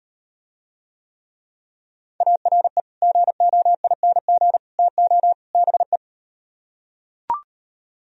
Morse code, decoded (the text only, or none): ARE GOING TO BE